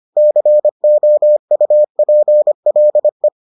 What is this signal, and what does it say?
Morse code, heard as a keyed tone, about 25 wpm, 600 Hz. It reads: COUPLE